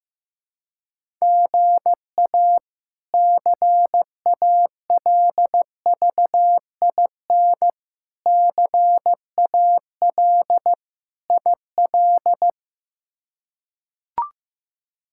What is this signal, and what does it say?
Morse code, heard as a keyed tone, about 15 wpm, 700 Hz. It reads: GA CALVIN CAL IL
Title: K1USN SST QSO Practice - 15wpm